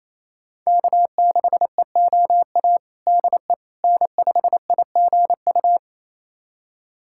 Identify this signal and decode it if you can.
Morse code, heard as a keyed tone, about 28 wpm, 700 Hz. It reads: K6EOA DE N5IGU
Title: USA Callsign DE Callsign - 5 Characters - 28wpm